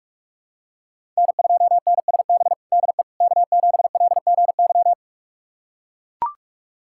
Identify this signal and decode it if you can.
Morse code, heard as a keyed tone, about 43 wpm, 700 Hz. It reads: N2NSB DE K7LGY
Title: USA Callsign DE Callsign - 5 Characters - 45wpm